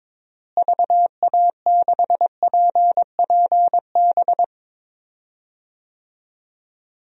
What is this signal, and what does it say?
Morse code, heard as a keyed tone, about 22 wpm, 700 Hz. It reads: VA6PPB
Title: DXCC-100 Call Signs - 6 Characters - Repeated 2x - 22wpm